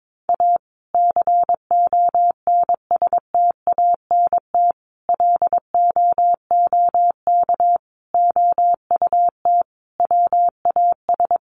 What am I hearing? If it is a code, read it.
Morse code, heard as a keyed tone, about 22 wpm, 700 Hz. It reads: A CONSTANT LOOK OUT WAS
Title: Sentences from Top 1500 Words 22wpm